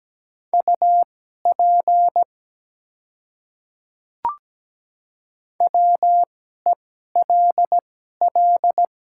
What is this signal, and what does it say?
Morse code, heard as a keyed tone, about 17 wpm, spaced as for 12 wpm, 700 Hz. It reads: UP WELL
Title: 2x Character Spacing - Top 200 Words - 17wpm